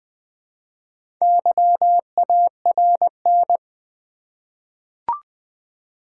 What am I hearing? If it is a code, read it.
Morse code, heard as a keyed tone, about 20 wpm, 700 Hz. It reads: YARN
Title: Easy Words with Hints - 20wpm